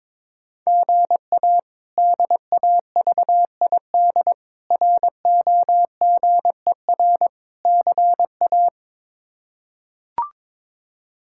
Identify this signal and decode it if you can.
Morse code, heard as a keyed tone, about 22 wpm, 700 Hz. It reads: GA DAVID ROGER CA
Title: K1USN SST QSO Practice - 22wpm